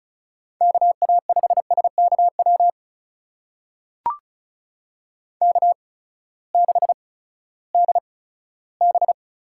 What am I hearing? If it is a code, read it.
Morse code, heard as a keyed tone, about 36 wpm, spaced as for 18 wpm, 700 Hz. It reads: KA5SKW K6DB
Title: USA Call Signs - 8x Character Spacing - 35wpm